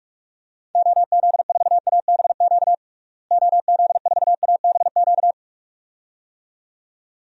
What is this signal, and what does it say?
Morse code, heard as a keyed tone, about 44 wpm, 700 Hz. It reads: OZ4ABQ OZ4ABQ
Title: DXCC-100 Call Signs - 6 Characters - Repeated 2x - 45wpm